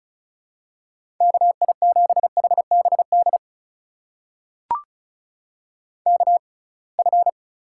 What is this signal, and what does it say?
Morse code, heard as a keyed tone, about 36 wpm, spaced as for 27 wpm, 700 Hz. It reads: KI7HBD KF
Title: USA Call Signs - 6 Characters - 6x Character Spacing - 35wpm